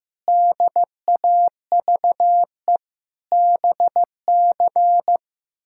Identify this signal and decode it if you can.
Morse code, heard as a keyed tone, about 15 wpm, 700 Hz. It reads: DAVE BC
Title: K1USN SST QSO Practice - 15wpm